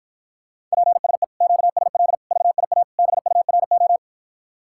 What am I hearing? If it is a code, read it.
Morse code, heard as a keyed tone, about 53 wpm, 700 Hz. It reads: PSE QSL VIA BURO